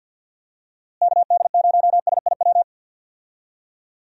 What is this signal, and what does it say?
Morse code, heard as a keyed tone, about 50 wpm, 700 Hz. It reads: KD0SIW
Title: USA Call Signs - 6 Characters - Repeated 2x - 50wpm